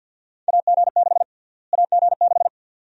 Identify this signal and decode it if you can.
Morse code, heard as a keyed tone, about 53 wpm, 700 Hz. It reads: AG6 AG6
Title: USA Call Sign Prefixes - Repeated 2x - 50wpm